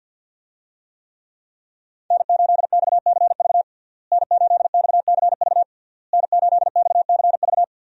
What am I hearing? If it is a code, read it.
Morse code, heard as a keyed tone, about 46 wpm, 700 Hz. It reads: N8XCV N8XCV N8XCV
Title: USA Call Signs - 5 Characters - Repeated 3x - 50wpm